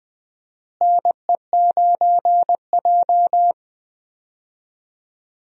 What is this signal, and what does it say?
Morse code, heard as a keyed tone, about 20 wpm, 700 Hz. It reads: NE9J
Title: USA Call Signs - 4 Characters - Repeated 3x - 20wpm